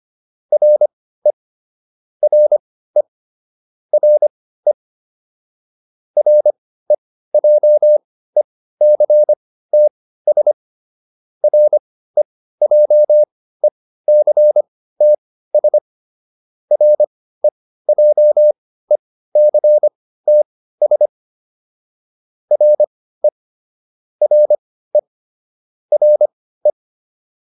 Morse code, heard as a keyed tone, about 25 wpm, spaced as for 15 wpm, 600 Hz. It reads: RE RE RE REJECTS REJECTS REJECTS RE RE RE